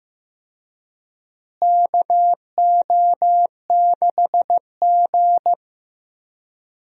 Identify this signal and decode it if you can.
Morse code, heard as a keyed tone, about 15 wpm, 700 Hz. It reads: KO6G
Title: USA Call Signs - 4 Characters - Repeated 3x - 15wpm